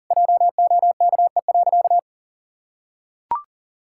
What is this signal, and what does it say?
Morse code, heard as a keyed tone, about 37 wpm, 700 Hz. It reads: JOKE.
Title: Pangram Challenge v2 - 40wpm